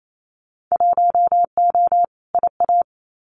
Morse code, heard as a keyed tone, about 28 wpm, 700 Hz. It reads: 1O IA
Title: Winter Field Day Exchanges 28wpm